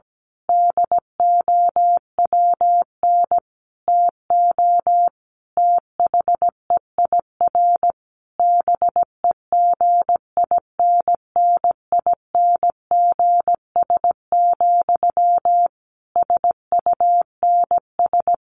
Morse code, heard as a keyed tone, about 17 wpm, 700 Hz. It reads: DOWN TO THEIR BEGINNINGS, SUNS
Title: A Sound Of Thunder 17wpm